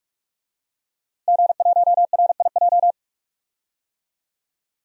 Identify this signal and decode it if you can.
Morse code, heard as a keyed tone, about 46 wpm, 700 Hz. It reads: G1RIJ
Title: DXCC-100 Call Signs - 5 Characters - 45wpm